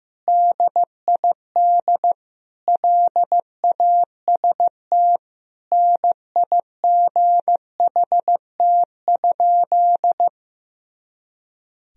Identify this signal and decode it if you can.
Morse code, heard as a keyed tone, about 16 wpm, 700 Hz. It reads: DID LAST NIGHT?
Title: Sentences from Top 300 Words and Abbreviations 15wpm